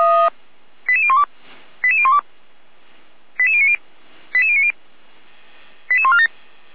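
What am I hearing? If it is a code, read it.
Five-tone selective call (ZVEI), heard as quick runs of five tones, tones of about 70 ms. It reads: 88012 88012 88090 88090 88147
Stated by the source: ZVEI five-tone selective call 88501 received by SDR, logged by Feuerwehr Orlamünde fire brigade.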